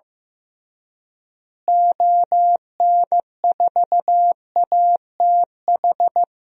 Morse code, heard as a keyed tone, about 16 wpm, 700 Hz. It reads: ON4ATH
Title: DXCC-100 Call Signs - 6 Characters - Repeated 2x - 15wpm